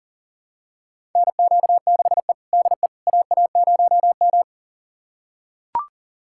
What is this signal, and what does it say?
Morse code, heard as a keyed tone, about 40 wpm, 700 Hz. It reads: NQ6E DE AA0M
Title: USA Callsign DE Callsign - 4 Characters - 40wpm